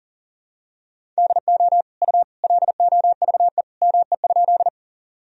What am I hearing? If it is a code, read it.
Morse code, heard as a keyed tone, about 40 wpm, 700 Hz. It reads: DO U LOVE ME?
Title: Sentences from Top 400 Words and Abbreviations 40wpm